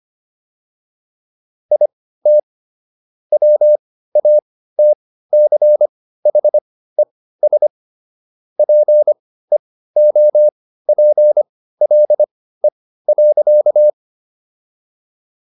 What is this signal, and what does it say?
Morse code, heard as a keyed tone, about 25 wpm, spaced as for 15 wpm, 600 Hz. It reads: IT WATCHES PEOPLE.